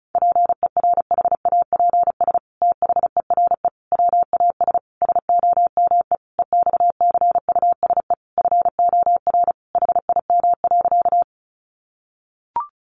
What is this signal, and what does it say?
Morse code, heard as a keyed tone, about 34 wpm, 700 Hz. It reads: PERHAPS THERE WAS SOME EXCUSE FOR HIM.
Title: Sentences from Top 700 Words 35wpm